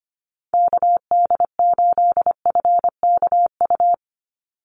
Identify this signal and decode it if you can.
Morse code, heard as a keyed tone, about 25 wpm, 700 Hz. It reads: KD8FKU
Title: USA Call Signs 25wpm